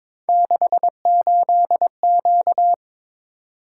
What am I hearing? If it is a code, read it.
Morse code, heard as a keyed tone, about 23 wpm, 700 Hz. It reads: 68Q
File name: practice / 22wpm / Three Letter-Number Combinations - 22wpm